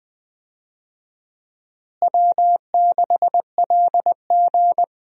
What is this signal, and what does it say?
Morse code, heard as a keyed tone, about 20 wpm, 700 Hz. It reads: W6LG
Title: USA Call Signs - 4 Characters - Repeated 3x - 20wpm